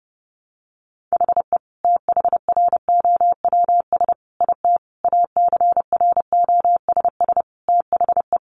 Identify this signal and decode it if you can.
Morse code, heard as a keyed tone, about 30 wpm, 700 Hz. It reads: HE THROWS IT ACROSS THE
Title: Additional Easy Phrases 30wpm